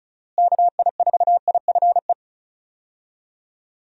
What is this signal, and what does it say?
Morse code, heard as a keyed tone, about 34 wpm, 700 Hz. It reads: KI4IFE
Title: USA Call Signs - 6 Characters - Repeated 3x - 35wpm